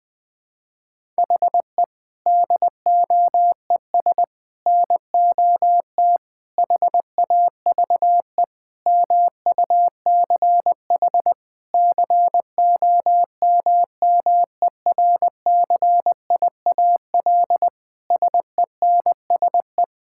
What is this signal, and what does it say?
Morse code, heard as a keyed tone, about 20 wpm, 700 Hz. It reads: HE DOES NOT HAVE MUCH COMMERCIAL SENSE
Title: Sentences from Top 1000 Words - Review Part 5 - 20wpm